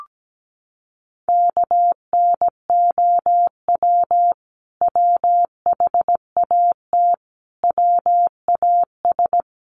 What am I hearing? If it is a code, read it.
Morse code, heard as a keyed tone, about 17 wpm, 700 Hz. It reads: KNOW WHAT WAS
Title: Sets of 5 Words Encore2 17wpm